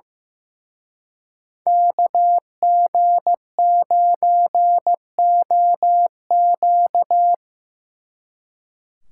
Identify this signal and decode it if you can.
Morse code, heard as a keyed tone, about 17 wpm, 700 Hz. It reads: KG9OQ
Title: Speed Racing - USA 2x2 Call Signs 15wpm